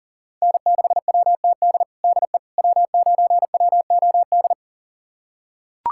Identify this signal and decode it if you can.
Morse code, heard as a keyed tone, about 40 wpm, 700 Hz. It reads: N6WTD DE W9WOD
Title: USA Callsign DE Callsign - 5 Characters - 40wpm